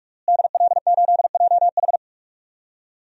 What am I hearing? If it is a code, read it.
Morse code, heard as a keyed tone, about 46 wpm, 700 Hz. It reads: DL8JH
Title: DXCC-100 Call Signs - 5 Characters - Repeated 3x - 45wpm